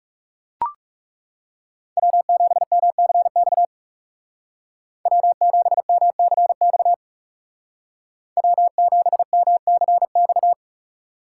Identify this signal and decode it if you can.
Morse code, heard as a keyed tone, about 37 wpm, 700 Hz. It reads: W7MCX W7MCX W7MCX
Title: Speed Racing - USA 1x3 Call Signs 30wpm